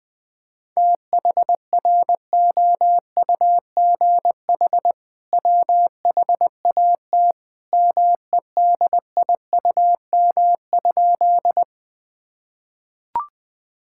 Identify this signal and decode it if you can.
Morse code, heard as a keyed tone, about 20 wpm, 700 Hz. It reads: THROUGH WHAT MEDIUM?
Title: Sentences from Top 1000 Words - Review Part 6 - 20wpm